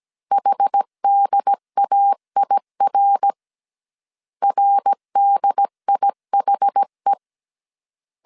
Morse code, heard as a keyed tone, about 17 wpm, 800 Hz. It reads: HDAIR RDIHE
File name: CW / DET / 1 3 1500C S A 16wpm 2141